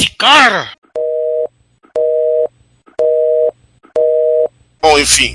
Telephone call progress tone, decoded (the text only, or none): busy tone